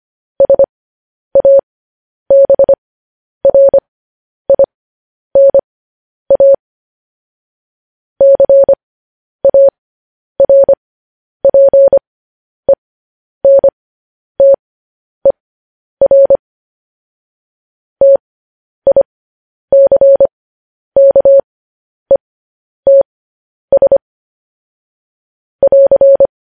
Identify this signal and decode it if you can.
Morse code, heard as a keyed tone, about 25 wpm, spaced as for 10 wpm, 550 Hz. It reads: SABRINA CARPENTER TICKETS <AR>
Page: news-headlines-in-morse-code-at-10-wpm